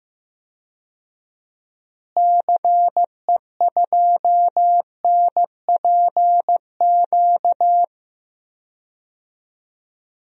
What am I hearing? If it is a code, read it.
Morse code, heard as a keyed tone, about 15 wpm, 700 Hz. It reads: CE2NPQ
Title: DXCC-100 Call Signs - 6 Characters - Repeated 3x - 15wpm